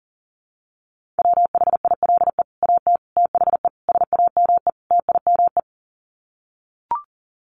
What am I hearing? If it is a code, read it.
Morse code, heard as a keyed tone, about 40 wpm, 700 Hz. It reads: WHILE AT THE SAME TIME
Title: Sets of 5 Words 40wpm